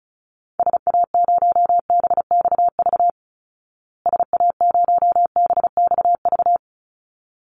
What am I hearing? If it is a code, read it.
Morse code, heard as a keyed tone, about 34 wpm, 700 Hz. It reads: SA0BXV SA0BXV
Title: Call Signs Repeated 3x - 35wpm